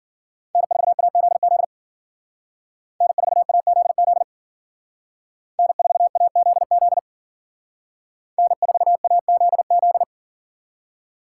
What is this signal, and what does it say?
Morse code, heard as a keyed tone, about 48 wpm, 700 Hz. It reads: N4AZZ N4AZZ N4AZZ N4AZZ
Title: Speed Racing - USA 1x3 Call Signs 40wpm